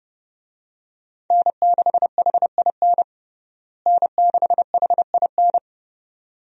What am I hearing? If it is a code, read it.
Morse code, heard as a keyed tone, about 31 wpm, 700 Hz. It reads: N6HIN N6HIN
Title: USA Call Signs - 5 Characters - Repeated 2x - 30wpm